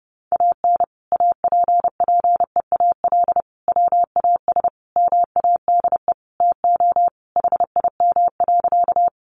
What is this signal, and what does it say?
Morse code, heard as a keyed tone, about 30 wpm, 700 Hz. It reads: AN APPEAL WAS MADE TO HIM.
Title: Sentences from Top 1300 Words 30wpm